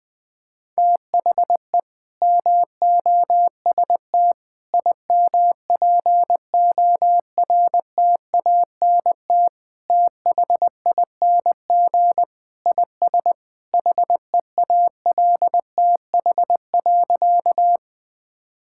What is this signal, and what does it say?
Morse code, heard as a keyed tone, about 20 wpm, 700 Hz. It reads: THE MOST IMPORTANT THING IS HEALTH.
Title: Sentences with 3-Word Phrases - 20wpm